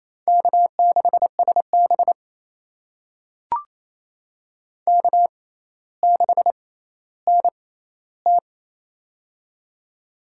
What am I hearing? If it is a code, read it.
Morse code, heard as a keyed tone, about 28 wpm, spaced as for 14 wpm, 700 Hz. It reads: K6SB K6NT
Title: USA Call Signs - 6x Character Spacing - 28wpm